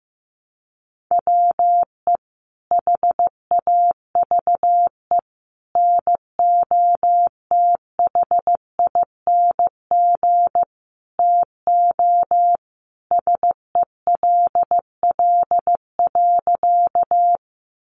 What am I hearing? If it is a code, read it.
Morse code, heard as a keyed tone, about 15 wpm, 700 Hz. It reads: WE HAVE NOTHING TO SELL.
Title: Sentences from Top 1100 Words 15wpm